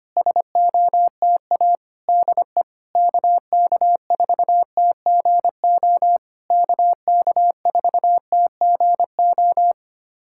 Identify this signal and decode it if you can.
Morse code, heard as a keyed tone, about 25 wpm, 690 Hz. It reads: SOTA DE KK4TGO KK4TGO
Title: SOTA QSO Practice 25wpm